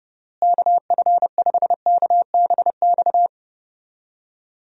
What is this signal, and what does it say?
Morse code, heard as a keyed tone, about 30 wpm, 700 Hz. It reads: KF5KBX
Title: USA Call Signs - 6 Characters - Repeated 2x - 30wpm